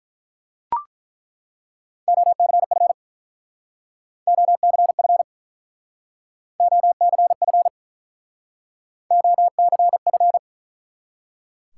Morse code, heard as a keyed tone, about 48 wpm, 700 Hz. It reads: OCF OCF OCF OCF